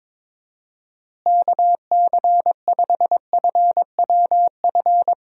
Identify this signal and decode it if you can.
Morse code, heard as a keyed tone, about 22 wpm, 700 Hz. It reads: KC5FWF
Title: USA Call Signs - 6 Characters - Repeated 2x - 22wpm